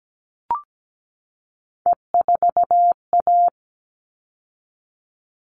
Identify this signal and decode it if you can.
Morse code, heard as a keyed tone, about 17 wpm, 700 Hz. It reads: E4A
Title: All Cut Numbers - 3-Digit 17wpm